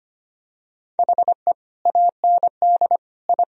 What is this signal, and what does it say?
Morse code, heard as a keyed tone, about 25 wpm, 700 Hz. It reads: HE AND I
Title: Sentences from Top 1000 Words - Review Part 1 - 25wpm